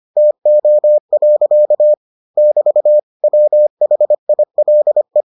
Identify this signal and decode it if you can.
Morse code, heard as a keyed tone, about 25 wpm, 600 Hz. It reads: TO. <BT> WHILE